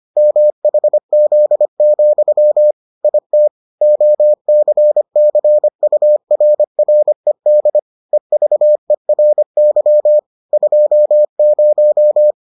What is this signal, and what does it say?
Morse code, heard as a keyed tone, about 25 wpm, 600 Hz. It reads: MHZ, IT OCCURRED EVERY 20